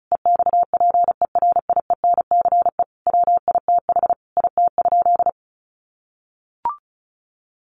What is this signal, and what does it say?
Morse code, heard as a keyed tone, about 34 wpm, 700 Hz. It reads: EXPERIENCE WITH IT?
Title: Sentences from Top 600 Words 35wpm